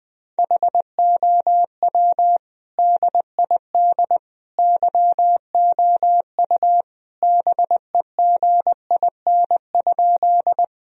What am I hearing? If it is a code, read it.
Morse code, heard as a keyed tone, about 21 wpm, 700 Hz. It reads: HOW DID YOU BEGIN?
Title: Sentences from Top 1000 Words - Review Part 1 - 20wpm